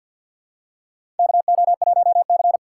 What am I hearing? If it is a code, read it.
Morse code, heard as a keyed tone, about 47 wpm, 700 Hz. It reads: KO1C
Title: USA Call Signs - 6x Character Spacing - 50wpm